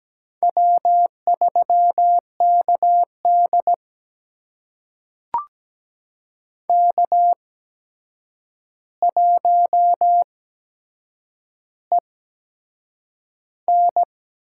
Morse code, heard as a keyed tone, about 17 wpm, spaced as for 7 wpm, 700 Hz. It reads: W3KD K1EN
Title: USA Call Signs - 4 Characters - 8x Character Spacing - 17wpm